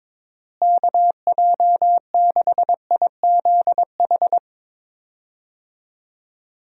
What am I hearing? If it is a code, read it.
Morse code, heard as a keyed tone, about 22 wpm, 700 Hz. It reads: KJ6IZH
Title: USA Call Signs - 6 Characters - Repeated 2x - 22wpm